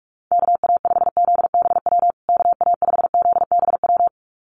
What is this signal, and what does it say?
Morse code, heard as a keyed tone, about 44 wpm, 700 Hz. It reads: KA5ZBW KA5ZBW
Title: USA Call Signs Repeated 2x 45wpm